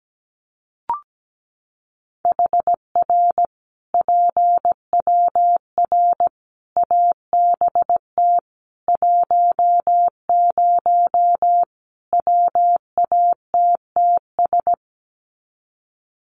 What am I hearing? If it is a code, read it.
Morse code, heard as a keyed tone, about 17 wpm, 700 Hz. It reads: HR PWR ABT 10 WATTS
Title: Common QSO Phrases 17wpm